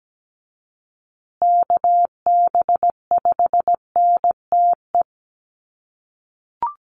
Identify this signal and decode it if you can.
Morse code, heard as a keyed tone, about 17 wpm, 700 Hz. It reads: KB5NTE
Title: USA Call Signs Repeated 3x 17wpm